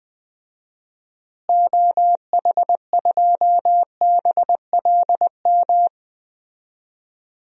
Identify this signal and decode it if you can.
Morse code, heard as a keyed tone, about 21 wpm, 690 Hz. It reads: OH2BLM
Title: DXCC-100 Call Signs - 6 Characters - Repeated 2x - 20wpm